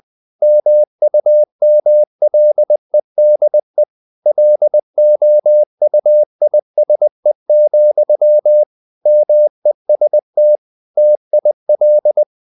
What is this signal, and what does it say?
Morse code, heard as a keyed tone, about 20 wpm, 600 Hz. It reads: MUMLEDE LOUISE, MEST TIL
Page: Ferie 20 wpm